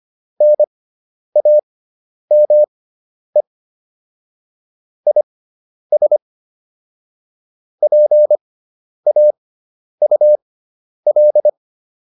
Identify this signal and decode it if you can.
Morse code, heard as a keyed tone, about 25 wpm, spaced as for 10 wpm, 600 Hz. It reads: NAME IS PAUL
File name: QSO407 10